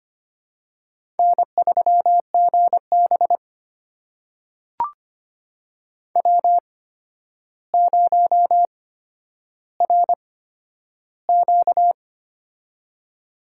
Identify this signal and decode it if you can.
Morse code, heard as a keyed tone, about 25 wpm, spaced as for 11 wpm, 700 Hz. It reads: N3GB W0RQ
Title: USA Call Signs - 4 Characters - 8x Character Spacing - 25wpm